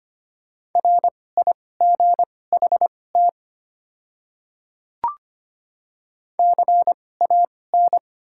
Morse code, heard as a keyed tone, about 25 wpm, spaced as for 18 wpm, 690 Hz. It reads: RIGHT CAN
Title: 2x Character Spacing - Top 200 Words - 25wpm